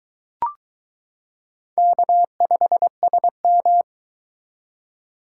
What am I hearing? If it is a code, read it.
Morse code, heard as a keyed tone, about 22 wpm, 700 Hz. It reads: K5SM